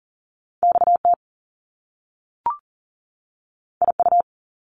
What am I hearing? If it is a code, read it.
Morse code, heard as a keyed tone, about 40 wpm, 700 Hz. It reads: XT IU
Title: Top Two Letter Combinations in Words 40wpm